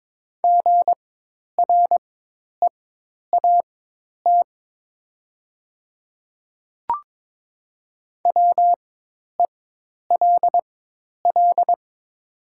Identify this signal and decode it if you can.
Morse code, heard as a keyed tone, about 22 wpm, spaced as for 10 wpm, 700 Hz. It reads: GREAT WELL
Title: 4x Character Spacing - Top 300 Words - 22wpm